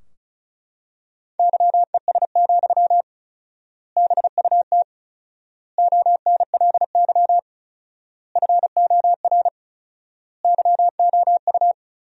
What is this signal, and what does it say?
Morse code, heard as a keyed tone, about 34 wpm, 700 Hz. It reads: YES, BUT ONLY FOR YOU
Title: 4x Word-Spacing - Sentences from Top 500 Words - 35wpm